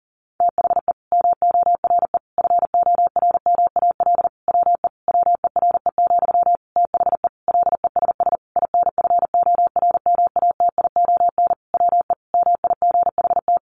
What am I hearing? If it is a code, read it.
Morse code, heard as a keyed tone, about 40 wpm, 700 Hz. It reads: THE MORE FORMAL WE WERE, THE LESS INFORMATION WE MIGHT
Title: Sentences from Top 1300 Words 40wpm